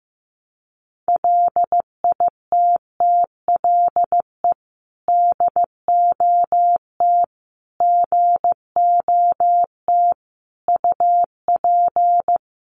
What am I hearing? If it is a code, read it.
Morse code, heard as a keyed tone, about 15 wpm, 700 Hz. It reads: LITTLE DOT GOT UP
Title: Sentences from Top 1500 Words 15wpm